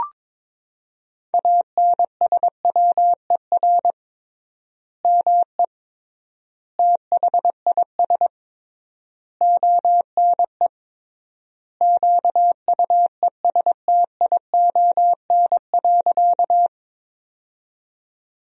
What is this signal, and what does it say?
Morse code, heard as a keyed tone, about 22 wpm, 700 Hz. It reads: ANSWER ME THIS ONE QUESTION.
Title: 3x Word-Spacing - Sentences from Top 400 Words - 22wpm